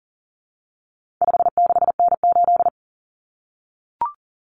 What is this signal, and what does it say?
Morse code, heard as a keyed tone, about 40 wpm, 700 Hz. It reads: 56N8
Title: Common Cut Numbers - 4-Digit 40wpm